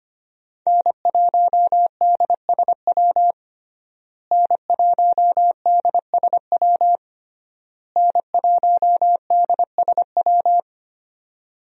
Morse code, heard as a keyed tone, about 25 wpm, 700 Hz. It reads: N1DSW N1DSW N1DSW
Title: USA Call Signs - 5 Characters - Repeated 3x - 25wpm